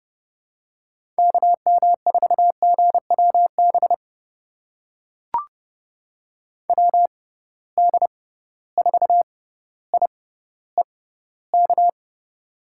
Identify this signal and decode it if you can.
Morse code, heard as a keyed tone, about 30 wpm, spaced as for 15 wpm, 700 Hz. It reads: KM4GWB WD4IEK
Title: USA Call Signs - 6x Character Spacing - 30wpm